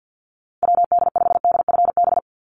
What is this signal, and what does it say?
Morse code, heard as a keyed tone, about 49 wpm, 700 Hz. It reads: WD5DFB
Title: USA Call Signs 50wpm